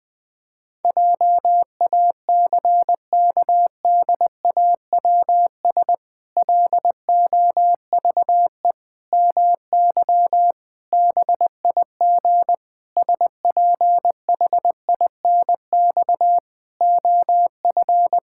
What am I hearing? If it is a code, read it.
Morse code, heard as a keyed tone, about 20 wpm, 700 Hz. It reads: JACKDAWS LOVE MY BIG SPHINX OF
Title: Pangram Challenge v2 - 20wpm